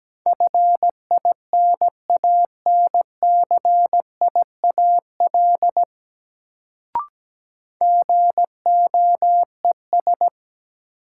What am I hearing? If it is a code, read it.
Morse code, heard as a keyed tone, about 17 wpm, 700 Hz. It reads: FINANCIAL GOES
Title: Sentences from Top 1000 Words - Review Part 5 - New Word Drill - 17wpm